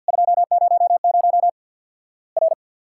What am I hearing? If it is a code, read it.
Morse code, heard as a keyed tone, about 50 wpm, 700 Hz. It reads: 200 R
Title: Worked All Europe DX Contest - QTC Exchange - 50wpm